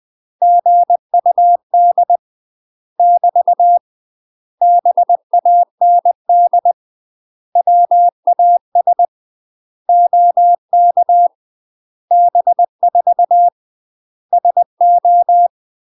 Morse code, heard as a keyed tone, about 20 wpm, 700 Hz. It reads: GUD <BT> BAND WAS OK B4 SO
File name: UK MASTER QSOs TRAINING / 20WPM / UK MASTER QSOs 0001